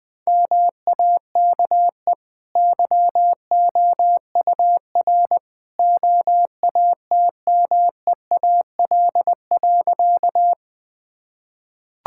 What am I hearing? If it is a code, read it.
Morse code, heard as a keyed tone, about 20 wpm, 700 Hz. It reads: MAKE YOUR OATMEAL.
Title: Easy Sentences with Hints - 20wpm